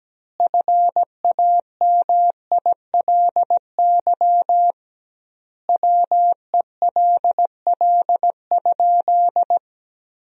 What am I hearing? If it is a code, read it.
Morse code, heard as a keyed tone, about 17 wpm, 700 Hz. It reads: FAMILY WELL?
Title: 2x Word-Spacing - Sentences from Top 400 Words - 17wpm